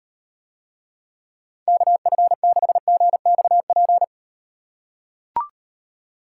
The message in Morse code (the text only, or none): KF6GXP